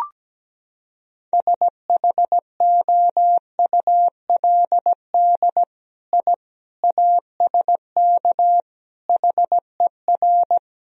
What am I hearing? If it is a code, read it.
Morse code, heard as a keyed tone, about 17 wpm, 700 Hz. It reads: SHOULD I ASK HER
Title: Sentences from Top 1000 Words - Review Part 4 - 17wpm